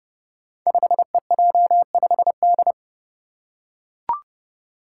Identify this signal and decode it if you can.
Morse code, heard as a keyed tone, about 28 wpm, 700 Hz. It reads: 5EJ5D
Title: Five Challenging Letter-Number Combinations 30wpm